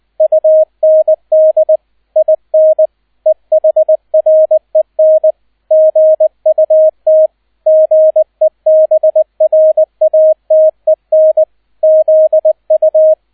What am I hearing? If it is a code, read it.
Morse code, heard as a keyed tone, about 20 wpm, 630 Hz, with radio noise. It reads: UND IN EHREN GUT GEBRATEN ZU
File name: 002 Zweiter Streich 20wpm